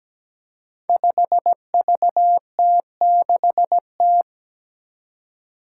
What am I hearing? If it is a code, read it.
Morse code, heard as a keyed tone, about 17 wpm, 700 Hz. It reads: 5VT6T
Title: Five Challenging Letter-Number Combinations 17wpm